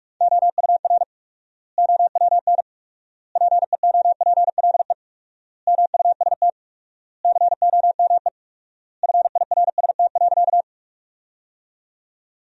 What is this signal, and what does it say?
Morse code, heard as a keyed tone, about 46 wpm, 700 Hz. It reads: OUR OWN PEOPLE MUST COME FIRST.
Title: 4x Word-Spacing - Sentences from Top 300 Words - 45wpm